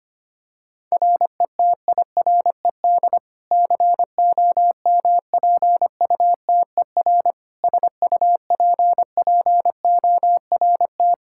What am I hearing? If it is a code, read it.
Morse code, heard as a keyed tone, about 25 wpm, 700 Hz. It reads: RETIRED COMPUTER SUPPORT
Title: Practice QSO - Exchange 3 - More Info - 25wpm